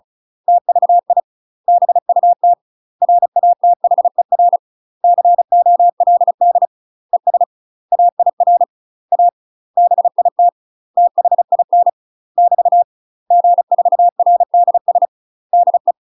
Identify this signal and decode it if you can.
Morse code, heard as a keyed tone, about 34 wpm, 700 Hz. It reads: TVI BUT RATHER COLD ES AIR A BIT THIN <BT> G4RDS DE
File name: UK MASTER QSOs TRAINING/35WPM